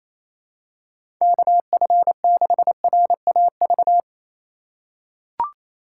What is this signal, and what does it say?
Morse code, heard as a keyed tone, about 28 wpm, 700 Hz. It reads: KF6RAV
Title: USA Call Signs - 6 Characters - 3x Character Spacing - 28wpm